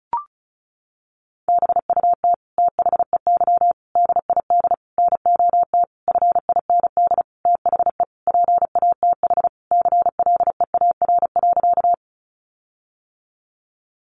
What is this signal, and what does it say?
Morse code, heard as a keyed tone, about 34 wpm, 700 Hz. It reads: BUT THEY DID NOT FIND THE PATH CLEAR.
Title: Sentences from Top 1400 Words 35wpm